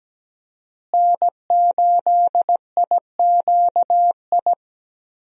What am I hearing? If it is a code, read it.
Morse code, heard as a keyed tone, about 17 wpm, 700 Hz. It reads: N8IQI
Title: USA Call Signs - 5 Characters - Repeated 2x - 17wpm